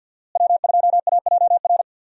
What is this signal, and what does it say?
Morse code, heard as a keyed tone, about 52 wpm, 700 Hz. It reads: W2AJR
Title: USA Call Signs - 5 Characters - Repeated 3x - 50wpm